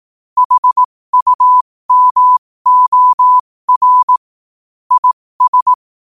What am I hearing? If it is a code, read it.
Morse code, heard as a keyed tone, about 18 wpm, spaced as for 15 wpm, 1000 Hz. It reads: HUMOR IS
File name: qotdfiles / 15wpm